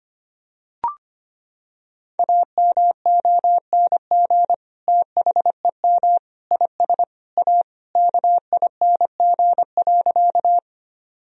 Morse code, heard as a keyed tone, about 25 wpm, 700 Hz. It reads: AMONG THEM IS A KING.
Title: Sentences from Top 1000 Words - Review Part 3 - 25wpm